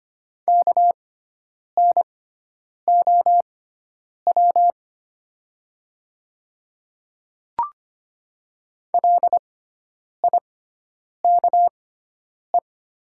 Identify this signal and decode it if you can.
Morse code, heard as a keyed tone, about 25 wpm, spaced as for 9 wpm, 700 Hz. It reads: KNOW LIKE
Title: 6x Character Spacing - Top 300 Words - 25wpm